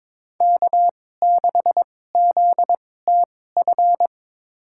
Morse code, heard as a keyed tone, about 22 wpm, spaced as for 16 wpm, 700 Hz. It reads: K6ZTF
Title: USA Call Signs - 5 Characters - 2x Character Spacing - 22wpm